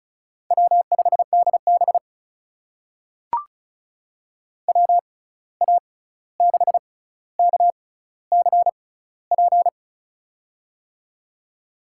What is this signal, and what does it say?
Morse code, heard as a keyed tone, about 36 wpm, spaced as for 16 wpm, 700 Hz. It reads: W5DB WA6KCP
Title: USA Call Signs - 6x Character Spacing - 35wpm